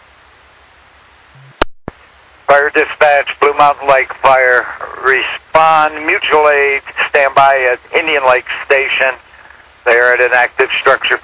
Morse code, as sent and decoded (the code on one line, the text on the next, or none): none